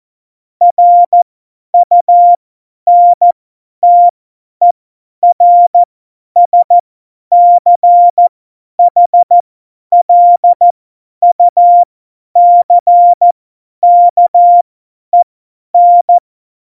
Morse code, heard as a keyed tone, about 14 wpm, spaced as for 10 wpm, 700 Hz. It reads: RUNTERSCHLUCKEN
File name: Morsekurs / Witze / Witz 13 March 2018, 50 F70